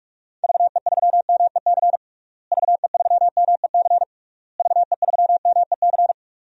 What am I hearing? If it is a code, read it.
Morse code, heard as a keyed tone, about 44 wpm, 700 Hz. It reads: VE3MEC VE3MEC VE3MEC
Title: DXCC-100 Call Signs - 6 Characters - Repeated 3x - 45wpm